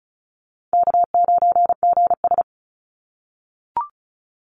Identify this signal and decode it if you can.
Morse code, heard as a keyed tone, about 36 wpm, 700 Hz. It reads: K9GS
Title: Call Signs - 35wpm